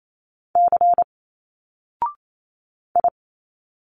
Morse code, heard as a keyed tone, about 28 wpm, 700 Hz. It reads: C I
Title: Single Letters 28wpm